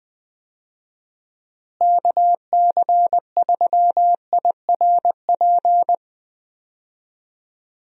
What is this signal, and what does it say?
Morse code, heard as a keyed tone, about 20 wpm, 700 Hz. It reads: KC3IRP
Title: USA Call Signs - 6 Characters - Repeated 3x - 20wpm